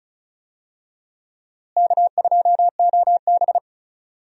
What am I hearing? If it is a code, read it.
Morse code, heard as a keyed tone, about 36 wpm, 700 Hz. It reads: K2OB
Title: USA Call Signs - 4 Characters - Repeated 2x - 35wpm